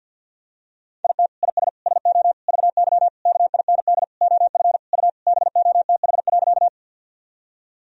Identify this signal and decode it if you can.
Morse code, heard as a keyed tone, about 52 wpm, 700 Hz. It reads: IT IS SO VY KIND OF U BOTH.
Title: Sentences from Top 300 Words and Abbreviations 50wpm